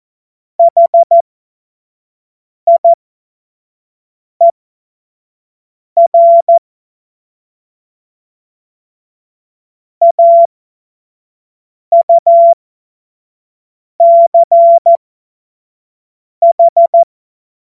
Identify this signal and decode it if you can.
Morse code, heard as a keyed tone, about 14 wpm, spaced as for 5 wpm, 680 Hz. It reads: HIER AUCH
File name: Morsekurs / Witze / Witz 06 Februar 2024 25 F70